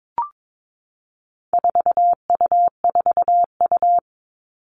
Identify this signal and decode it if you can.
Morse code, heard as a keyed tone, about 22 wpm, 700 Hz. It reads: 4U4U
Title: All Cut Numbers - 4-Digit 22wpm